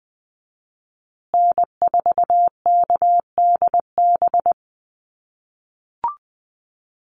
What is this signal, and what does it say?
Morse code, heard as a keyed tone, about 20 wpm, 700 Hz. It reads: N4KDB
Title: USA Call Signs Repeated 2x 20wpm